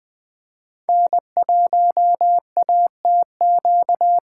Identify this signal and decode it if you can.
Morse code, heard as a keyed tone, about 21 wpm, 700 Hz. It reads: N1ATQ
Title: Speed Racing - USA Call Signs 17wpm